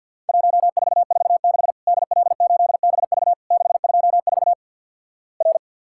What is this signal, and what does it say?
Morse code, heard as a keyed tone, about 50 wpm, 700 Hz. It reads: 1446 DL8BV 624 R
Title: Worked All Europe DX Contest - QTC Exchange with Repeats - 50wpm